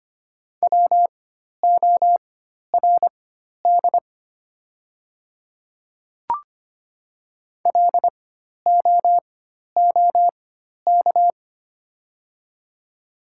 Morse code, heard as a keyed tone, about 25 wpm, spaced as for 12 wpm, 700 Hz. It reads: WORD LOOK
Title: 4x Character Spacing - Top 200 Words - 25wpm